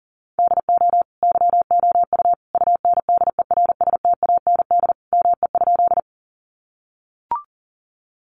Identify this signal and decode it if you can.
Morse code, heard as a keyed tone, about 40 wpm, 700 Hz. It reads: DO YOU UNDERSTAND ME?
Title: Sentences from Top 600 Words 40wpm